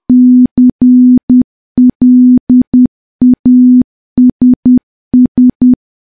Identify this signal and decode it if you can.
Morse code, heard as a keyed tone, about 10 wpm, 250 Hz. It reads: CLASS
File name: morse / morse-training-10WPM